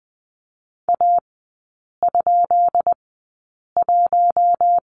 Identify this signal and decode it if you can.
Morse code, heard as a keyed tone, about 20 wpm, spaced as for 8 wpm, 690 Hz. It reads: A?1
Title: 133 Lesson 22e - Question Mark - Three Characters 20wpm